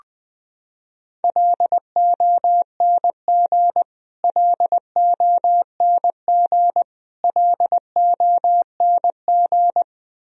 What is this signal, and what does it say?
Morse code, heard as a keyed tone, about 20 wpm, 700 Hz. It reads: LONG LONG LONG
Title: Top 100 Words Repeated 20wpm